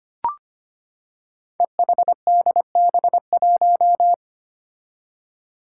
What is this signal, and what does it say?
Morse code, heard as a keyed tone, about 25 wpm, 700 Hz. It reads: EHDB1
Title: Five Challenging Letter-Number Combinations 25wpm